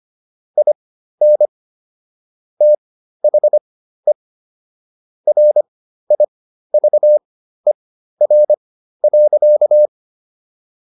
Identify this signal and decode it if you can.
Morse code, heard as a keyed tone, about 25 wpm, spaced as for 13 wpm, 600 Hz. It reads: IN THE RIVER.